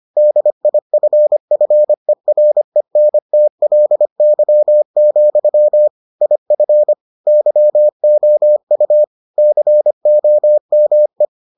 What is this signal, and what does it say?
Morse code, heard as a keyed tone, about 25 wpm, 600 Hz. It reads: DIFFERENTLY, IF YOU COME